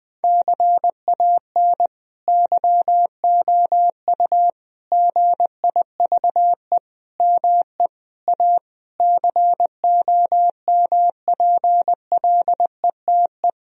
Morse code, heard as a keyed tone, about 20 wpm, 700 Hz. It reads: CAN YOU GIVE ME A COMPLETE
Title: Sentences from Top 1000 Words - Review Part 2 - 20wpm